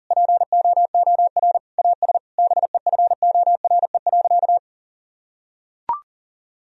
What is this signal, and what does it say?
Morse code, heard as a keyed tone, about 40 wpm, 700 Hz. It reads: POOR AS BEFORE.
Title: Sentences from Top 1000 Words - Review Part 2 - 40wpm